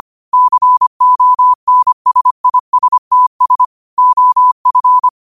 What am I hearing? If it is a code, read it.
Morse code, heard as a keyed tone, about 25 wpm, 1000 Hz. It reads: CONSISTS OF
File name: qotdfiles / 25wpm